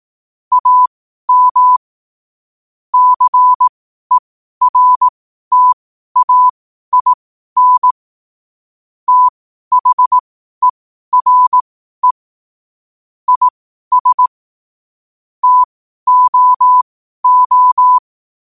Morse code, heard as a keyed tone, about 18 wpm, spaced as for 12 wpm, 1000 Hz. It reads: AM CERTAIN THERE IS TOO